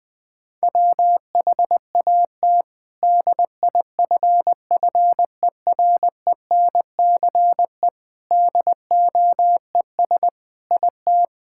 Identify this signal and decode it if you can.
Morse code, heard as a keyed tone, about 21 wpm, 700 Hz. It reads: WHAT DIFFERENCE DOES IT
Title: Sentences from Top 1000 Words - Review Part 5 - 20wpm